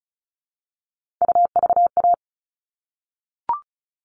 Morse code, heard as a keyed tone, about 34 wpm, 700 Hz. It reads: UVA
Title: All Cut Numbers - 3-Digit 35wpm